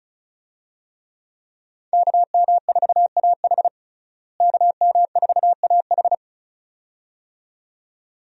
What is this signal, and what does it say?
Morse code, heard as a keyed tone, about 36 wpm, 700 Hz. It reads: KM4AH KM4AH
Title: USA Call Signs - 5 Characters - Repeated 2x - 35wpm